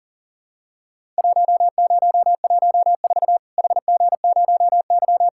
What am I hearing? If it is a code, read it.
Morse code, heard as a keyed tone, about 40 wpm, 700 Hz. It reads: 1014 HG0Y
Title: Worked All Europe DX Contest - QTC Exchange - 40wpm